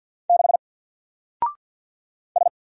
Morse code, heard as a keyed tone, about 50 wpm, 700 Hz. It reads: 6 S
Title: Single Letter-Number 50wpm